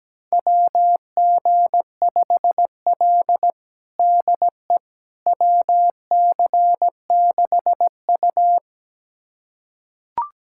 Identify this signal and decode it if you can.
Morse code, heard as a keyed tone, about 17 wpm, 700 Hz. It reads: WG5L DE WC6U
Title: USA Callsign DE Callsign - 4 Characters - 17wpm